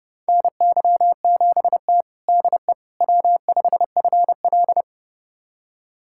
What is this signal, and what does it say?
Morse code, heard as a keyed tone, about 30 wpm, 700 Hz. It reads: NY7T DE W5FL
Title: USA Callsign DE Callsign - 4 Characters - 30wpm